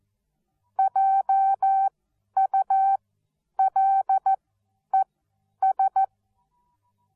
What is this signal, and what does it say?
Morse code, heard as a keyed tone, about 14 wpm, spaced as for 10 wpm, 780 Hz. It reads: JULES